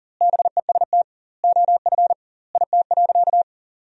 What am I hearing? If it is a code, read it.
Morse code, heard as a keyed tone, about 40 wpm, 700 Hz. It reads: BEST OF IT.
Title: 2x Word-Spacing - Sentences from Top 400 Words - 40wpm